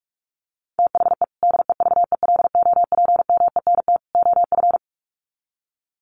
Morse code, heard as a keyed tone, about 44 wpm, 700 Hz. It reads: THE DEVELOPMENT OF
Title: Sets of 3 Words 45wpm